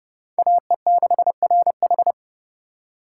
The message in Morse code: AE6RH

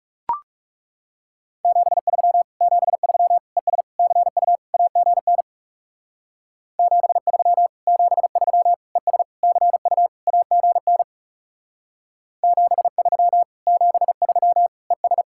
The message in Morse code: 73 73 ES CU AGN 73 73 ES CU AGN 73 73 ES

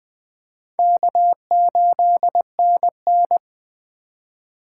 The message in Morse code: K8NN

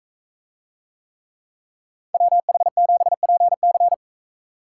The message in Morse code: WH7PC

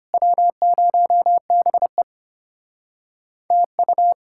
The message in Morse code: W0BE TU

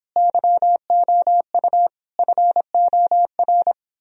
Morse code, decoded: YOU FOR

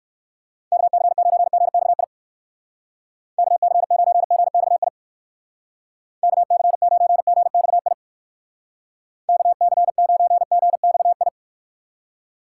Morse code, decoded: KC9GXI KC9GXI KC9GXI KC9GXI

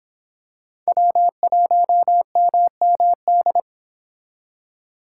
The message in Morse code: W1MMD

